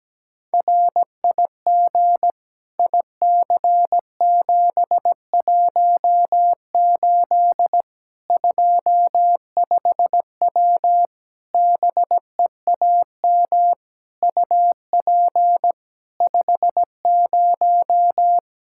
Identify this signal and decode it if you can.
Morse code, heard as a keyed tone, about 17 wpm, 700 Hz. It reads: RIG IC718 25W BEAM UP 50
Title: K1USN SST QSO Practice 17wpm